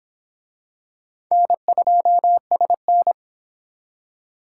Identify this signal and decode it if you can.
Morse code, heard as a keyed tone, about 23 wpm, 700 Hz. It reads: N2SN